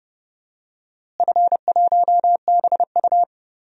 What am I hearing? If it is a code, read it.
Morse code, heard as a keyed tone, about 30 wpm, 700 Hz. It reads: F1BU
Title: DXCC-100 Call Signs - 4 Characters - Repeated 3x - 30wpm